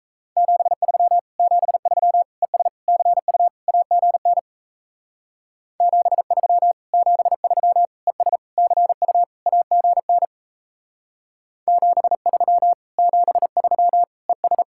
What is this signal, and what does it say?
Morse code, heard as a keyed tone, about 35 wpm, 700 Hz. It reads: 73 73 ES CU AGN 73 73 ES CU AGN 73 73 ES